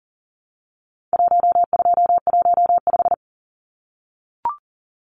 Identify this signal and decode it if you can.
Morse code, heard as a keyed tone, about 40 wpm, 700 Hz. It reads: 1215
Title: Numbers - 4-Digit 40wpm